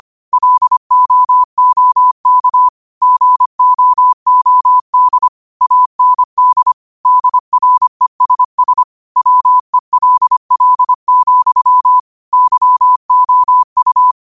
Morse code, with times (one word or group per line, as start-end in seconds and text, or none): 0.3-2.7 LOOK
3.0-5.3 GOOD
5.6-6.7 AND
7.1-8.8 DRESS
9.2-12.0 WELL,
12.3-14.1 YOU